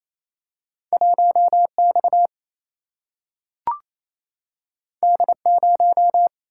Morse code, heard as a keyed tone, about 28 wpm, 700 Hz. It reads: 1 X D 0
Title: Single-Letter Single-Number 28wpm